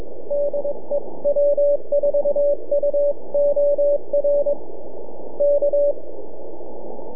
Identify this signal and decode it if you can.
Morse code, heard as a keyed tone, about 22 wpm, 580 Hz, with radio noise. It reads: DE W4UOR K
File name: NAQPCW-8-2009 / W1UJ NAQP CW 09 1800Z